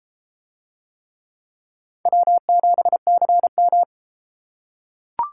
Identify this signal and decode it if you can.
Morse code, heard as a keyed tone, about 32 wpm, 700 Hz. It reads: W7CM